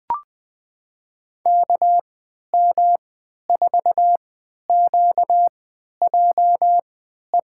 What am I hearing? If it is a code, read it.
Morse code, heard as a keyed tone, about 21 wpm, spaced as for 11 wpm, 700 Hz. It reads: KM4QJE
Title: USA Call Signs - 3x Character Spacing - 20wpm